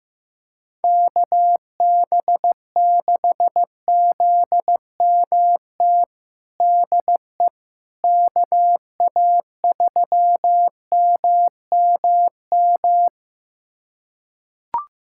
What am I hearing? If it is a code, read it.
Morse code, heard as a keyed tone, about 15 wpm, 700 Hz. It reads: KB6ZMT DE KA3MMM
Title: USA Callsign DE Callsign 15wpm